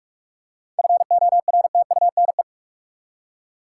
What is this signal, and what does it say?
Morse code, heard as a keyed tone, about 46 wpm, 700 Hz. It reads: FORTUNE
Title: Seven Letter Words - 45wpm